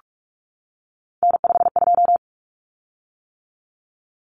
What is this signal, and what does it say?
Morse code, heard as a keyed tone, about 44 wpm, 700 Hz. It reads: N52
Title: Common Cut Numbers - 3-Digit 45wpm